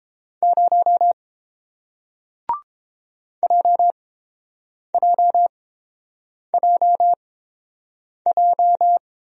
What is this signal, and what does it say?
Morse code, heard as a keyed tone, about 31 wpm, spaced as for 7 wpm, 700 Hz. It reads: 0 JJJJ